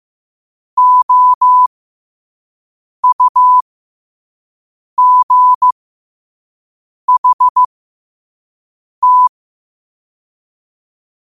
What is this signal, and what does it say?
Morse code, heard as a keyed tone, about 15 wpm, spaced as for 5 wpm, 1000 Hz. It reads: OUGHT
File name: qotdfiles / 5wpm